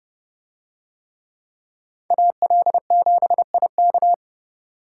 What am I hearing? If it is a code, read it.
Morse code, heard as a keyed tone, about 29 wpm, 700 Hz. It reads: AL7IK